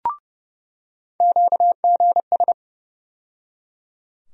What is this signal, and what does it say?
Morse code, heard as a keyed tone, about 30 wpm, 700 Hz. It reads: QGS
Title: USA Call Sign Postfixes - 3 Letters 30wpm